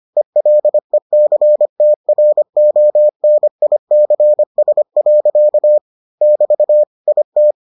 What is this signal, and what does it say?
Morse code, heard as a keyed tone, about 25 wpm, 600 Hz. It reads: ELECTRONICS. <BT> IT